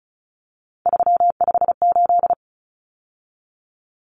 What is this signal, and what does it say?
Morse code, heard as a keyed tone, about 34 wpm, 700 Hz. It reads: 358
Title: Numbers - 3-Digit 35wpm